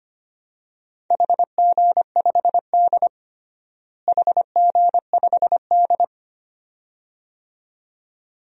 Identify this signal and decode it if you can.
Morse code, heard as a keyed tone, about 25 wpm, 700 Hz. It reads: HG5D HG5D
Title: DXCC-100 Call Signs - 4 Characters - Repeated 2x - 25wpm